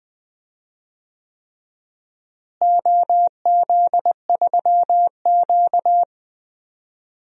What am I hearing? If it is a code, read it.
Morse code, heard as a keyed tone, about 20 wpm, 700 Hz. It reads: OZ3Q
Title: DXCC-100 Call Signs - 4 Characters - Repeated 3x - 20wpm